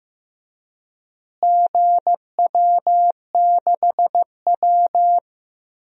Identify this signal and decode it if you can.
Morse code, heard as a keyed tone, about 16 wpm, 700 Hz. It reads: GW6W
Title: DXCC-100 Call Signs - 4 Characters - Repeated 2x - 15wpm